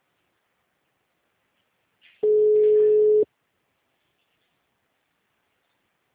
Ringback tone (Europe, 425 Hz).